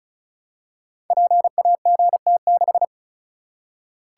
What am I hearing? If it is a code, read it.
Morse code, heard as a keyed tone, about 36 wpm, 700 Hz. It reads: PAGT6
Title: Code Groups - 5 Letter:Numbers - 35wpm